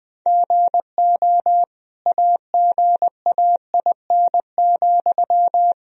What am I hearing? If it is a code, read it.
Morse code, heard as a keyed tone, about 20 wpm, 700 Hz. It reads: GO AGAIN,
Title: Sentences from Top 1000 Words - Review Part 4 - 20wpm